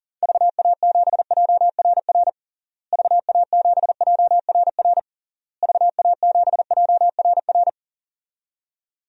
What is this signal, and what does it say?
Morse code, heard as a keyed tone, about 40 wpm, 700 Hz. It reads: VA7JRR VA7JRR VA7JRR
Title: DXCC-100 Call Signs - 6 Characters - Repeated 3x - 40wpm